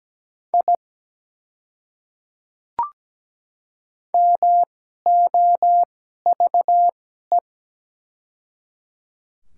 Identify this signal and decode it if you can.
Morse code, heard as a keyed tone, about 17 wpm, spaced as for 12 wpm, 700 Hz. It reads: I MOVE
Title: 2x Character Spacing - Top 300 Words - 17wpm